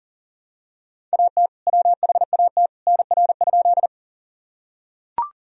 Morse code, heard as a keyed tone, about 40 wpm, 700 Hz. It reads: AT WHAT NR?
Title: Sentences from Top 100 Words and Abbreviations 40wpm